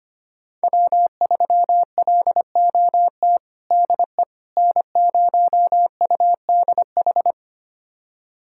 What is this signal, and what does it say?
Morse code, heard as a keyed tone, about 25 wpm, 700 Hz. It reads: W3LOT DE N0UDH
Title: USA Callsign DE Callsign - 5 Characters - 25wpm